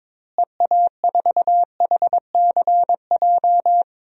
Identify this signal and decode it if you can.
Morse code, heard as a keyed tone, about 21 wpm, 700 Hz. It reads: EA4HCJ
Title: DXCC-100 Call Signs - 6 Characters - Repeated 2x - 22wpm